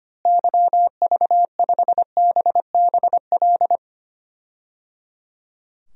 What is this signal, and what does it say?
Morse code, heard as a keyed tone, about 25 wpm, 700 Hz. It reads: YV5BBL
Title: DXCC-100 Call Signs - 6 Characters - Repeated 3x - 25wpm